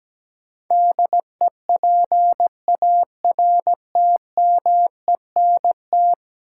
Morse code, heard as a keyed tone, about 17 wpm, 700 Hz. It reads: DEPARTMENT
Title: Sentences from Top 1200 Words - New Word Drill Alt - 17wpm